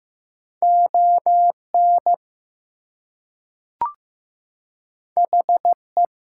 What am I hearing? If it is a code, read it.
Morse code, heard as a keyed tone, about 15 wpm, 700 Hz. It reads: ON HE
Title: Two Letter Words - 15wpm